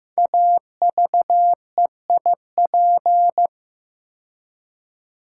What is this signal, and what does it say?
Morse code, heard as a keyed tone, about 15 wpm, 700 Hz. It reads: AVEIP
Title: Code Groups - 5 Letters - 15wpm